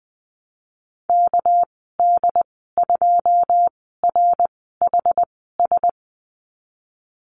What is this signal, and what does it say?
Morse code, heard as a keyed tone, about 20 wpm, spaced as for 15 wpm, 700 Hz. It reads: KD2RHS
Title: 219 Lesson 35g - Number 0 - Call Signs 20wpm